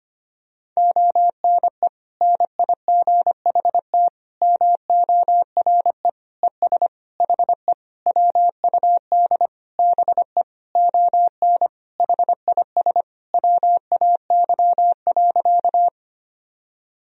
Morse code, heard as a keyed tone, about 25 wpm, 700 Hz. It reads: ONE NIGHT MORE ES HE WUD BE ON HIS WAY.
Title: Sentences from Top 300 Words and Abbreviations 25wpm